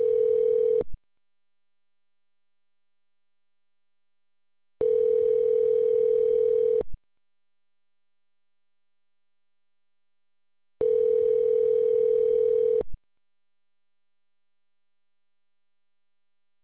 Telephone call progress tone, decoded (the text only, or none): ringback tone